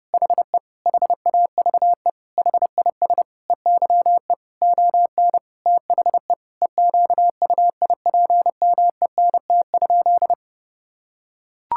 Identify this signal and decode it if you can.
Morse code, heard as a keyed tone, about 30 wpm, 700 Hz. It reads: HE HAVE HIS EYE ON THE EQUIPMENT?
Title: Sentences from Top 1000 Words - Review Part 5 - 30wpm